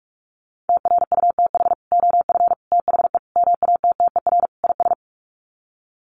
Morse code, heard as a keyed tone, about 44 wpm, 700 Hz. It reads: TRUTH OF THE MATTER IS
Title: Sets of 5 Words Encore2 45wpm